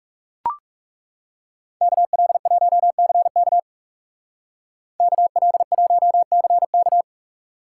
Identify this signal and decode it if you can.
Morse code, heard as a keyed tone, about 35 wpm, 700 Hz. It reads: KL1CK KL1CK
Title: Speed Racing - USA 2x2 Call Signs 30wpm